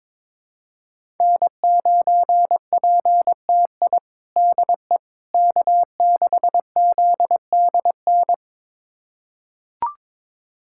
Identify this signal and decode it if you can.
Morse code, heard as a keyed tone, about 22 wpm, 700 Hz. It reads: N9PTI DE K6ZDN
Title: USA Callsign DE Callsign - 5 Characters - 22wpm